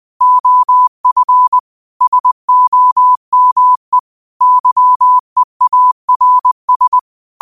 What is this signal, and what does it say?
Morse code, heard as a keyed tone, about 20 wpm, 1000 Hz. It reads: OF SOME YEARS
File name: qotdfiles / 20wpm